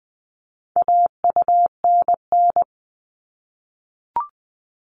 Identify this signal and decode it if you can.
Morse code, heard as a keyed tone, about 20 wpm, 700 Hz. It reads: AUNN
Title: All Cut Numbers - 4-Digit 20wpm